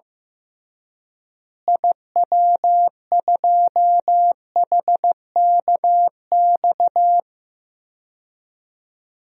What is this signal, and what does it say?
Morse code, heard as a keyed tone, about 15 wpm, 700 Hz. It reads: IW2HKX
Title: DXCC-100 Call Signs - 6 Characters - Repeated 3x - 15wpm